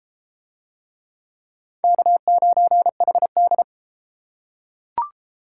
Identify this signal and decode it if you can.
Morse code, heard as a keyed tone, about 33 wpm, 700 Hz. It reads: K9HD